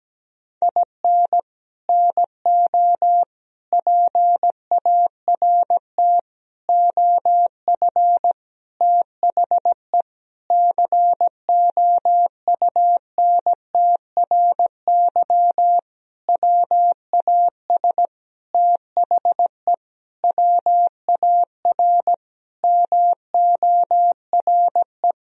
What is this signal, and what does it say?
Morse code, heard as a keyed tone, about 17 wpm, 700 Hz. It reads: IN NO PART OF THE COUNTRY WAS THE WAR MORE
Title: Sentences from Top 1000 Words - Review Part 3 - 17wpm